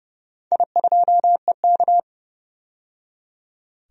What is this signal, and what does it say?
Morse code, heard as a keyed tone, about 30 wpm, 700 Hz. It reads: I2EK
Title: DXCC-100 Call Signs - 4 Characters - 30wpm